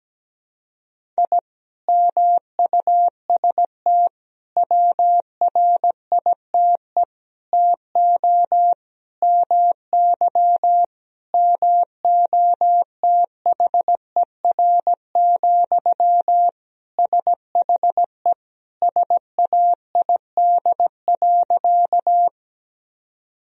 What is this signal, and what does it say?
Morse code, heard as a keyed tone, about 17 wpm, 700 Hz. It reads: I MUST WRITE TO MY MOTHER, SHE SAID.
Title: Sentences from Top 1000 Words - Review Part 1 - 17wpm